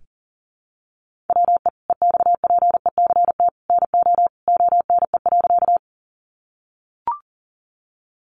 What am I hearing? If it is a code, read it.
Morse code, heard as a keyed tone, about 40 wpm, 700 Hz. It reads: WE EXPECT NO ONE.
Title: Sentences from Top 1100 Words 40wpm